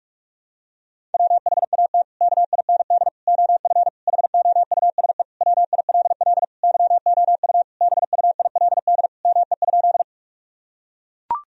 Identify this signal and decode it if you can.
Morse code, heard as a keyed tone, about 41 wpm, 700 Hz. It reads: WHAT KIND OF HOUSE WILL YOU BUILD ME?
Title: Sentences from Top 1000 Words - Review Part 1 - 45wpm